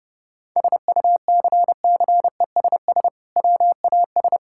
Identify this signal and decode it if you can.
Morse code, heard as a keyed tone, about 32 wpm, 700 Hz. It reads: SUCCESS WAS
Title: Sentences from Top 1000 Words - Review Part 6 - 30wpm